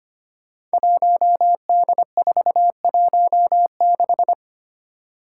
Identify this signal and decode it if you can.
Morse code, heard as a keyed tone, about 25 wpm, 700 Hz. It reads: 1D416
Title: Five Challenging Letter-Number Combinations 25wpm